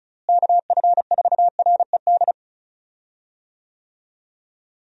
Morse code, heard as a keyed tone, about 36 wpm, 700 Hz. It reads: KF4RED
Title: USA Call Signs - 6 Characters - Repeated 3x - 35wpm